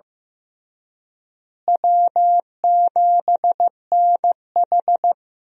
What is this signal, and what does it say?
Morse code, heard as a keyed tone, about 15 wpm, 700 Hz. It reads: W7NH
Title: USA Call Signs - 4 Characters - Repeated 3x - 15wpm